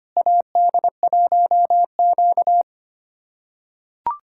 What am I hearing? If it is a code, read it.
Morse code, heard as a keyed tone, about 25 wpm, 700 Hz. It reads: AD1Q
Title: USA Callsign DE Callsign - 4 Characters - 25wpm